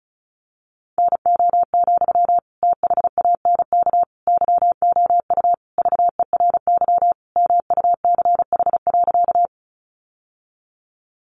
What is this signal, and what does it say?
Morse code, heard as a keyed tone, about 34 wpm, 700 Hz. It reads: NO, THANK YOU VERY MUCH.
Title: Sentences from Top 600 Words 35wpm